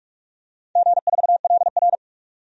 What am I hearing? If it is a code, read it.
Morse code, heard as a keyed tone, about 44 wpm, 700 Hz. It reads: G4LR
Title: DXCC-100 Call Signs - 4 Characters - 45wpm